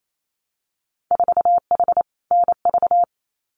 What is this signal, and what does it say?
Morse code, heard as a keyed tone, about 28 wpm, 700 Hz. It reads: 4H NV
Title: Winter Field Day Exchanges Encore 28wpm